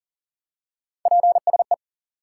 Morse code, heard as a keyed tone, about 40 wpm, 700 Hz. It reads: PSE
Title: Top 90 QSO Elements 40wpm